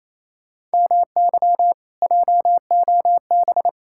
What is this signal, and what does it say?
Morse code, heard as a keyed tone, about 38 wpm, spaced as for 33 wpm, 700 Hz. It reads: MY JOB
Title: Speed-Racing Common QSO Phrases - 28wpm